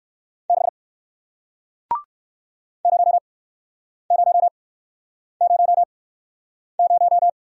Morse code, heard as a keyed tone, about 56 wpm, spaced as for 9 wpm, 700 Hz. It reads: 6 0000
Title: Speed-Racing Single Letter-Number 45wpm